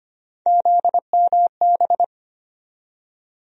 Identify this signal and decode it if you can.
Morse code, heard as a keyed tone, about 25 wpm, 700 Hz. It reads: ZMB
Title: USA Call Sign Postfixes - 3 Letters 25wpm